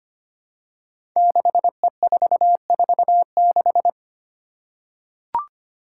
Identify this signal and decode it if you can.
Morse code, heard as a keyed tone, about 25 wpm, 700 Hz. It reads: 6E446
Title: Five Challenging Letter-Number Combinations 25wpm